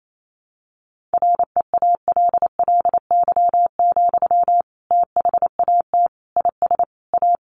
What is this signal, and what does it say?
Morse code, heard as a keyed tone, about 28 wpm, 700 Hz. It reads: REALLY, THAT IS A
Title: Sentences from Top 600 Words 28wpm